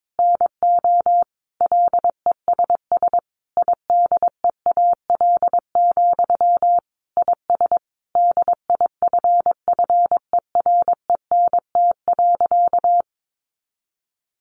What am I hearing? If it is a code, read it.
Morse code, heard as a keyed tone, about 22 wpm, 700 Hz. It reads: NO LESS IDEAL, IS DIFFERENT.
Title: Sentences from Top 1400 Words 22wpm